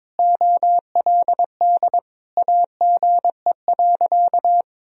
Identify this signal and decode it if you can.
Morse code, heard as a keyed tone, about 21 wpm, 690 Hz. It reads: OLD AGE.
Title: Sentences from Top 1000 Words - Review Part 3 - 22wpm